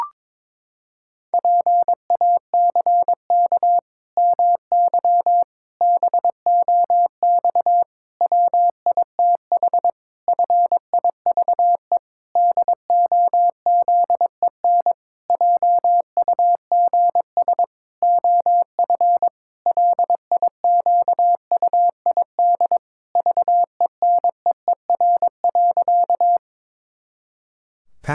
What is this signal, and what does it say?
Morse code, heard as a keyed tone, about 23 wpm, 700 Hz. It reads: PACK MY BOX WITH FIVE DOZEN JUGS OF LIQUID VENEER.
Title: Pangram Challenge v2 - 22wpm